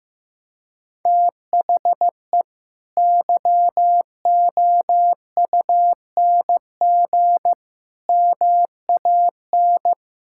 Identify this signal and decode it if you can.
Morse code, heard as a keyed tone, about 16 wpm, 700 Hz. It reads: THE YOUNG MAN
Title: Sentences from Top 1000 Words - Review Part 6 - 15wpm